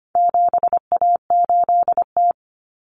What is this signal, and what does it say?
Morse code, heard as a keyed tone, about 25 wpm, 700 Hz. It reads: 7A8T
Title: Common Cut Numbers - 4-Digit 25wpm